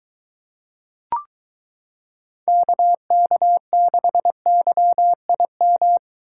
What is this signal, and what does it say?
Morse code, heard as a keyed tone, about 23 wpm, 700 Hz. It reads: KK6YIM